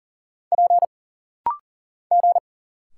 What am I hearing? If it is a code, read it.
Morse code, heard as a keyed tone, about 40 wpm, 700 Hz. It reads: P G
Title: Single Letters - Rapid-Fire 40wpm